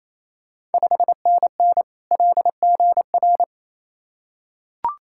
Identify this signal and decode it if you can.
Morse code, heard as a keyed tone, about 28 wpm, 700 Hz. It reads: 5NN LGR
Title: Ontario QSO Party Exchange 28wpm